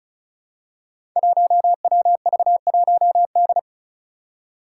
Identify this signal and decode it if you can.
Morse code, heard as a keyed tone, about 36 wpm, 700 Hz. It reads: 1WV1D
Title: Five Challenging Letter-Number Combinations 35wpm